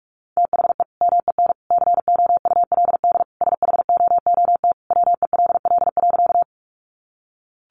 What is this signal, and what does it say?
Morse code, heard as a keyed tone, about 44 wpm, 700 Hz. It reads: THE MEN COULD SHOOT WELL.
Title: Sentences from Top 1200 Words 45wpm